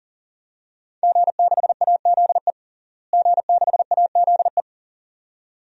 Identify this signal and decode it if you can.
Morse code, heard as a keyed tone, about 40 wpm, 700 Hz. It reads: G6AZE G6AZE
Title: DXCC-100 Call Signs - 5 Characters - Repeated 2x - 40wpm